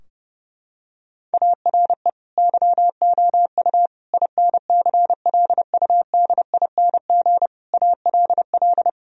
Morse code, heard as a keyed tone, about 29 wpm, 700 Hz. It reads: ARE YOU INCLUDING ALL
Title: Sentences from Top 1000 Words - Review Part 4 - 30wpm